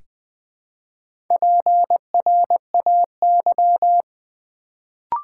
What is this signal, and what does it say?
Morse code, heard as a keyed tone, about 20 wpm, 700 Hz. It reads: PRAY
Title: Build A Word - 1 Letter At A Time - 4-Letter Words - 20wpm